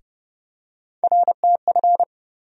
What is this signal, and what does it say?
Morse code, heard as a keyed tone, about 32 wpm, 700 Hz. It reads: RTF
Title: USA Call Sign Postfixes - 3 Letters 30wpm